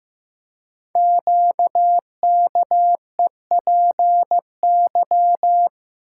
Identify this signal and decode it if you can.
Morse code, heard as a keyed tone, about 15 wpm, 700 Hz. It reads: QKEPY
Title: Code Groups - 5 Letters - 15wpm